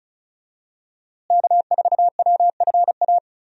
Morse code, heard as a keyed tone, about 36 wpm, 700 Hz. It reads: K4WFA
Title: USA Call Signs - 5 Characters - Repeated 2x - 35wpm